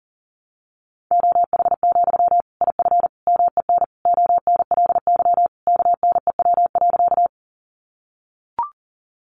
Morse code, heard as a keyed tone, about 40 wpm, 700 Hz. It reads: OH, IF MEN ONLY KNEW.
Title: Sentences from Top 500 Words 40wpm